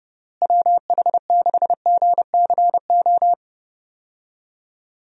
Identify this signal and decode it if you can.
Morse code, heard as a keyed tone, about 30 wpm, 700 Hz. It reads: WH6GCO
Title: USA Call Signs - 6 Characters - Repeated 3x - 30wpm